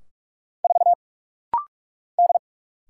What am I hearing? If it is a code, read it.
Morse code, heard as a keyed tone, about 48 wpm, 690 Hz. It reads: 4 D
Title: Single Letter-Number - Rapid-Fire 45wpm